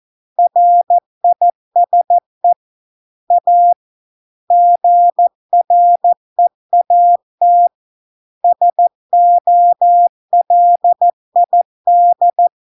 Morse code, heard as a keyed tone, about 14 wpm, 700 Hz. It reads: RISE A GREAT SOLID